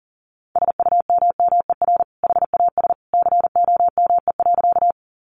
Morse code, heard as a keyed tone, about 40 wpm, 700 Hz. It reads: SUMMER HAS COME.
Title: Sentences from Top 1100 Words 40wpm